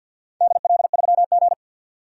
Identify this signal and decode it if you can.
Morse code, heard as a keyed tone, about 46 wpm, 700 Hz. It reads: DL3G
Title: DXCC-100 Call Signs - 4 Characters - 50wpm